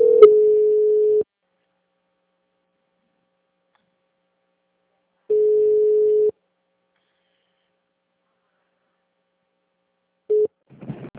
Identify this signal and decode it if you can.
Ringback tone (Europe, 425 Hz).
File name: out- -2014- - - .31528